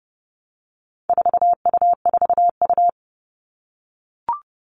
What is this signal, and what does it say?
Morse code, heard as a keyed tone, about 30 wpm, 700 Hz. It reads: 4U4U
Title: All Cut Numbers - 4-Digit 30wpm